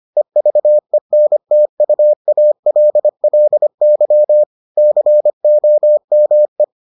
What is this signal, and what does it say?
Morse code, heard as a keyed tone, about 25 wpm, 600 Hz. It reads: EVENTUALLY COME